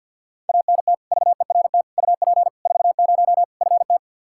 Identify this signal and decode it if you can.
Morse code, heard as a keyed tone, about 52 wpm, 700 Hz. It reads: ANT VERT UP 40 FT
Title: Practice QSO - Exchange 2 - Info - 50wpm